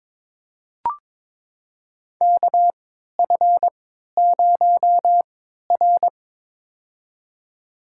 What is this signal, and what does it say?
Morse code, heard as a keyed tone, about 22 wpm, spaced as for 13 wpm, 700 Hz. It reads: KF0R
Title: USA Call Signs - 4 Characters - 3x Character Spacing - 22wpm